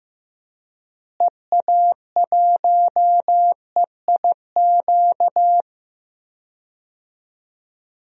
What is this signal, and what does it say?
Morse code, heard as a keyed tone, about 15 wpm, 700 Hz. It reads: EA1EIQ
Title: DXCC-100 Call Signs - 6 Characters - Repeated 3x - 15wpm